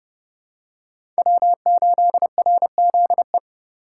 Morse code, heard as a keyed tone, about 31 wpm, 700 Hz. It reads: W8RZE
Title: USA Call Signs - 5 Characters - 2x Character Spacing - 30wpm